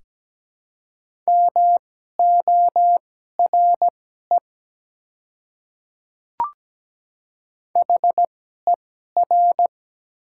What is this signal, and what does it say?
Morse code, heard as a keyed tone, about 17 wpm, spaced as for 12 wpm, 700 Hz. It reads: MORE HER
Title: 2x Character Spacing - Top 100 Words - 17wpm